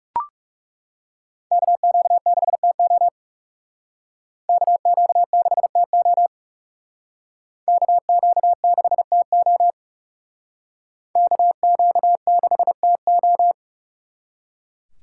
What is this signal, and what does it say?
Morse code, heard as a keyed tone, about 36 wpm, 700 Hz. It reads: KQ6TO KQ6TO KQ6TO KQ6TO
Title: Speed Racing - USA 2x2 Call Signs 30wpm